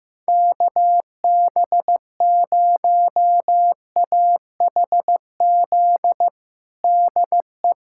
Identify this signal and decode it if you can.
Morse code, heard as a keyed tone, about 15 wpm, 700 Hz. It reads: KB0AHZ DE
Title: USA Callsign DE Callsign - 6 Characters - 15wpm